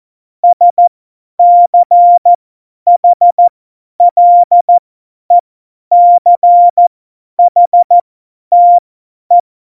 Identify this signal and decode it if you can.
Morse code, heard as a keyed tone, about 14 wpm, spaced as for 10 wpm, 700 Hz. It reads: SCHLECHTE
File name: Morsekurs / Witze / Witz 17.Dezember 2019, 50 F70